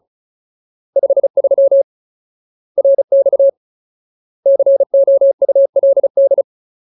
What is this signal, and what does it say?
Morse code, heard as a keyed tone, about 34 wpm, 550 Hz. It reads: 53 RX COULD
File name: Ultimate Word List / 35WPM Ultimate